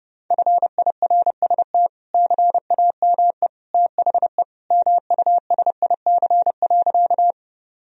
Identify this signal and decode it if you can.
Morse code, heard as a keyed tone, about 32 wpm, 700 Hz. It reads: FIRST CAME THE MUSIC.
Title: Sentences from Top 1000 Words - Review Part 1 - 30wpm